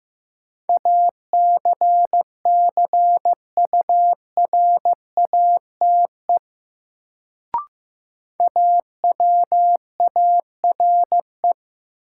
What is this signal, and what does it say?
Morse code, heard as a keyed tone, about 15 wpm, 700 Hz. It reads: ACCURATE AWARE
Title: Sentences from Top 1200 Words - New Word Drill - 15wpm